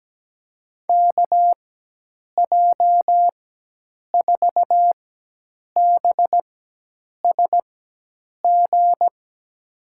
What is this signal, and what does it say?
Morse code, heard as a keyed tone, about 17 wpm, spaced as for 8 wpm, 700 Hz. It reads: KJ4BSG
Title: USA Call Signs - 6 Characters - 4x Character Spacing - 17wpm